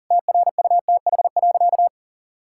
Morse code, heard as a keyed tone, about 40 wpm, 700 Hz. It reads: TRUTH.
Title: Sentences from Top 1000 Words - Review Part 6 - 40wpm